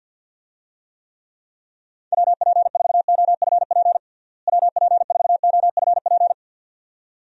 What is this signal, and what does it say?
Morse code, heard as a keyed tone, about 46 wpm, 700 Hz. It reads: WP4OFP WP4OFP
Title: USA Call Signs - 6 Characters - Repeated 2x - 50wpm